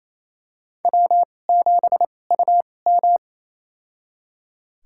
Morse code, heard as a keyed tone, about 28 wpm, spaced as for 20 wpm, 700 Hz. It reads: W7UM
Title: USA Call Signs - 4 Characters - 2x Character Spacing - 28wpm